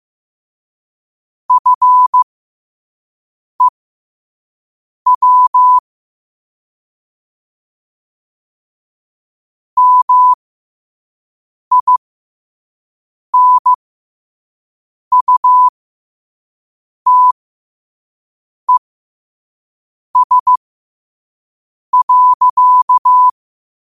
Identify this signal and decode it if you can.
Morse code, heard as a keyed tone, about 15 wpm, spaced as for 5 wpm, 1000 Hz. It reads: FEW MINUTES.
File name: qotdfiles / 5wpm